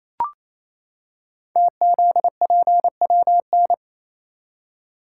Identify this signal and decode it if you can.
Morse code, heard as a keyed tone, about 28 wpm, 700 Hz. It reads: TZPWN
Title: Code Groups - 5 Letter:Numbers - 28wpm